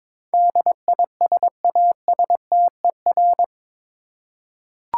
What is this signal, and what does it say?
Morse code, heard as a keyed tone, about 22 wpm, 700 Hz. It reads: DISASTER
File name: practice / 22wpm / Eight Letter Words - 22wpm